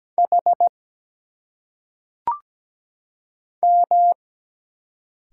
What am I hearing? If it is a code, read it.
Morse code, heard as a keyed tone, about 17 wpm, 700 Hz. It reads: H M
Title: Single Letters - Phonetic Pronunciation 17wpm